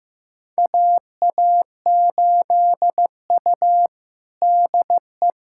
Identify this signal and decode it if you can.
Morse code, heard as a keyed tone, about 15 wpm, 700 Hz. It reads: AA8U DE
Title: USA Callsign DE Callsign 15wpm